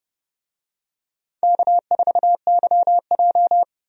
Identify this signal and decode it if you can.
Morse code, heard as a keyed tone, about 30 wpm, 700 Hz. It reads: K4YJ
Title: USA Call Signs - 4 Characters - 30wpm